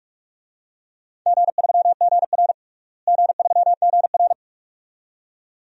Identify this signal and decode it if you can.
Morse code, heard as a keyed tone, about 46 wpm, 700 Hz. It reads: G3GR G3GR
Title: DXCC-100 Call Signs - 4 Characters - Repeated 2x - 45wpm